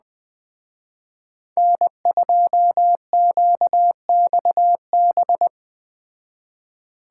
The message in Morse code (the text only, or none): N2QXB